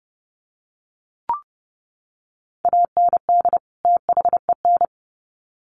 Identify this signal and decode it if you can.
Morse code, heard as a keyed tone, about 30 wpm, 700 Hz. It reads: AND THEN
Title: Sets of 2 Words 30wpm